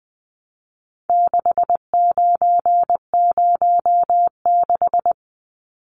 Morse code, heard as a keyed tone, about 20 wpm, 700 Hz. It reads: 6906
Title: Numbers - 4-Digit 20wpm